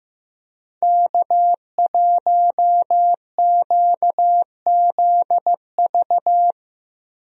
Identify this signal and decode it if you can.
Morse code, heard as a keyed tone, about 15 wpm, 700 Hz. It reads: K1QZV
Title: USA Call Signs - 5 Characters - Repeated 2x - 15wpm